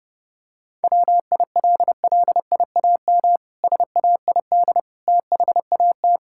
Morse code, heard as a keyed tone, about 30 wpm, 700 Hz. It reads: WILLIAM SAID THAT
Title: Pangram Challenge v2 - 30wpm